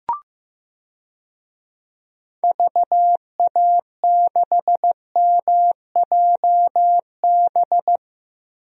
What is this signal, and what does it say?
Morse code, heard as a keyed tone, about 15 wpm, 700 Hz. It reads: VA6MJB
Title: DXCC-100 Call Signs - 6 Characters - Repeated 2x - 15wpm